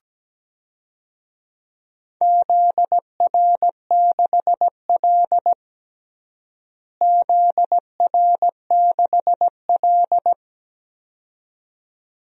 Morse code, heard as a keyed tone, about 17 wpm, 700 Hz. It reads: ZR6L ZR6L
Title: DXCC-100 Call Signs - 4 Characters - Repeated 2x - 17wpm